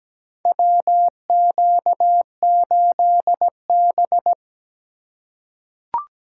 Morse code, heard as a keyed tone, about 17 wpm, 700 Hz. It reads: WQ8B
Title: USA Call Signs - 4x Character Spacing - 17wpm